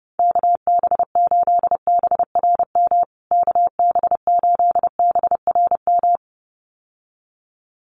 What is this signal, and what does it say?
Morse code, heard as a keyed tone, about 30 wpm, 700 Hz. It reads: KB8BRM KB8BRM
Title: USA Call Signs Repeated 2x 30wpm